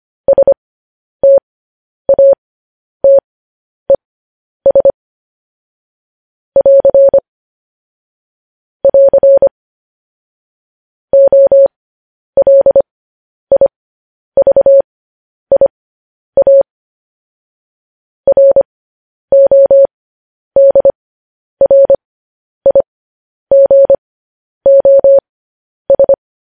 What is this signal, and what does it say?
Morse code, heard as a keyed tone, about 25 wpm, spaced as for 10 wpm, 550 Hz. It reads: STATES <AR> <AR> OLIVIA RODRIGOS
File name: podcast / News headlines-1x-25x10